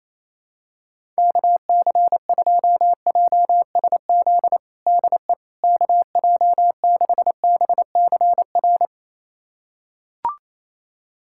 Morse code, heard as a keyed tone, about 28 wpm, 700 Hz. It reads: KC2JSZ DE KJ6BCR
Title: USA Callsign DE Callsign - 6 Characters - 28wpm